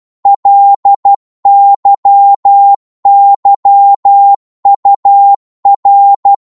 Morse code, heard as a keyed tone, about 12 wpm, 800 Hz. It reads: LYYUR
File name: Morse / ltr-12wpm